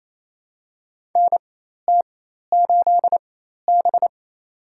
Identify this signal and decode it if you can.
Morse code, heard as a keyed tone, about 28 wpm, spaced as for 13 wpm, 700 Hz. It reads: NT8B
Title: USA Call Signs - 4 Characters - 4x Character Spacing - 28wpm